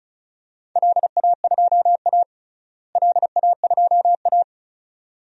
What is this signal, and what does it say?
Morse code, heard as a keyed tone, about 34 wpm, 700 Hz. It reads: LA2A LA2A
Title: DXCC-100 Call Signs - 4 Characters - Repeated 2x - 35wpm